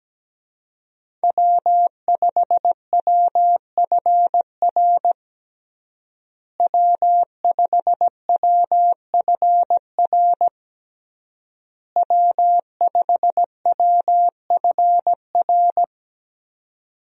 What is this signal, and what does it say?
Morse code, heard as a keyed tone, about 17 wpm, 700 Hz. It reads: W5WFR W5WFR W5WFR
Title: USA Call Signs - 5 Characters - Repeated 3x - 17wpm